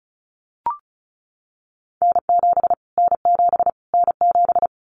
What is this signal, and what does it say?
Morse code, heard as a keyed tone, about 34 wpm, 700 Hz. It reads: N7 N7 N7
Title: Callsign Prefixes for Top 25 DXCC - Repeated 3x - 35wpm